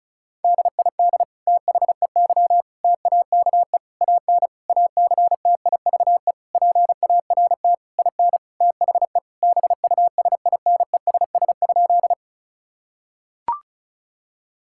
Morse code, heard as a keyed tone, about 36 wpm, 700 Hz. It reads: DID THEY TAKE AN ACTIVE PART IN THE BUSINESS?
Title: Sentences from Top 1000 Words - Review Part 6 - 35wpm